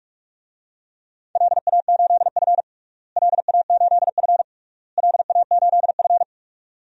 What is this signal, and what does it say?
Morse code, heard as a keyed tone, about 42 wpm, 700 Hz. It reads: LA8F LA8F LA8F
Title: DXCC-100 Call Signs - 4 Characters - Repeated 3x - 45wpm